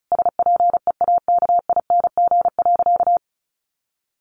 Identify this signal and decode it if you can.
Morse code, heard as a keyed tone, about 34 wpm, 700 Hz. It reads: SPEAKING.
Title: A Sound Of Thunder 35wpm